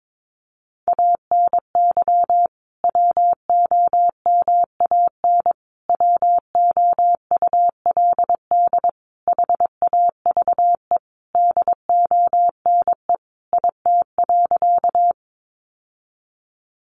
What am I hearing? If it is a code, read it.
Morse code, heard as a keyed tone, about 22 wpm, 700 Hz. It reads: ANY WOMAN WOULD HAVE DONE IT.
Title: Sentences from Top 1100 Words 22wpm